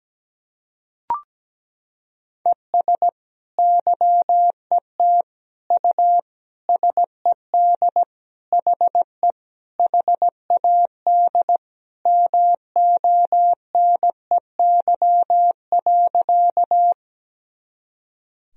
Morse code, heard as a keyed tone, about 17 wpm, 700 Hz. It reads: ES YET U SED HE HAD MONEY.
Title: Sentences from Top 500 Words and Abbreviations 17wpm